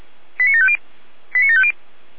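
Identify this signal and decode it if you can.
Five-tone selective call (ZVEI), heard as quick runs of five tones, tones of about 70 ms. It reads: 89750 89750
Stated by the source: ZVEI five-tone selective call, address 89380, received by SDR.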